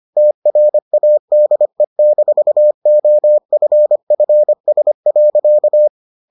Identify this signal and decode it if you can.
Morse code, heard as a keyed tone, about 25 wpm, 600 Hz. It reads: TRADE-OFFS.